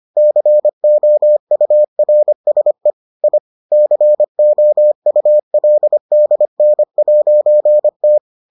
Morse code, heard as a keyed tone, about 25 wpm, 600 Hz. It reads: COURSE I COULDN'T